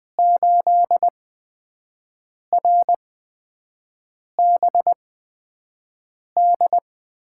Morse code, heard as a keyed tone, about 20 wpm, spaced as for 5 wpm, 700 Hz. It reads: 8RBD
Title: USA Call Signs - 8x Character Spacing - 20wpm